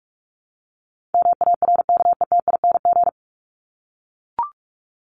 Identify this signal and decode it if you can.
Morse code, heard as a keyed tone, about 44 wpm, 700 Hz. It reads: MARKETING
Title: Top 751-1000 Words 45wpm